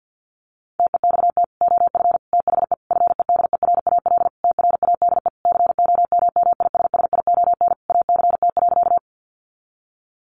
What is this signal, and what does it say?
Morse code, heard as a keyed tone, about 49 wpm, 700 Hz. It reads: TEXT OF THE FEDERAL TRADE COMMISSION ACT.
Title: Sentences from Top 1400 Words 50wpm